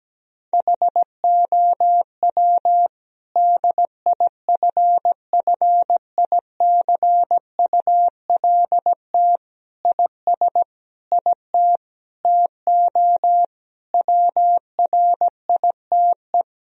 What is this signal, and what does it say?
Morse code, heard as a keyed tone, about 17 wpm, 700 Hz. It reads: HOW DIFFICULT IS IT TO WRITE
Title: Sentences from Top 1000 Words - Review Part 3 - 17wpm